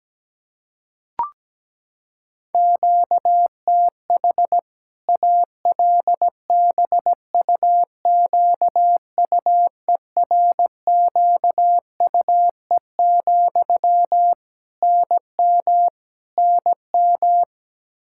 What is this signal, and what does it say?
Morse code, heard as a keyed tone, about 17 wpm, 700 Hz. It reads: QTH ALBUQUERQUE, NM NM
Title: Practice QSO - Exchange 1 - Beginning - 17wpm